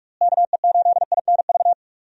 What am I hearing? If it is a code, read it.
Morse code, heard as a keyed tone, about 46 wpm, 700 Hz. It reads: KE8INV
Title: USA Call Signs - 6 Characters - Repeated 3x - 45wpm